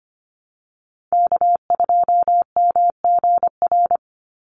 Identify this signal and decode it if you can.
Morse code, heard as a keyed tone, about 25 wpm, 700 Hz. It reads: K2MGR
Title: Call Signs - 25wpm